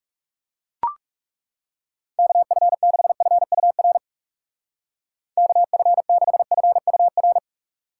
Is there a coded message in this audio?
Yes: Morse code from 2.2 to 7.4 s, heard as a keyed tone, about 39 wpm, 700 Hz. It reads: KF6FUR KF6FUR